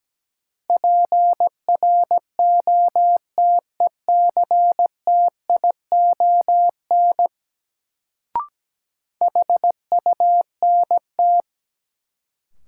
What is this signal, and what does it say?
Morse code, heard as a keyed tone, about 17 wpm, 700 Hz. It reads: PROTECTION HUNT
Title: Sentences from Top 1600 Words - New Word Drill Alt - 17wpm